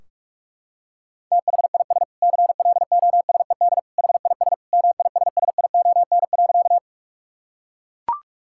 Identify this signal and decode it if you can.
Morse code, heard as a keyed tone, about 42 wpm, 700 Hz. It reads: THIS CLOSED HIS MISSION.
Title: Sentences from Top 1600 Words 45wpm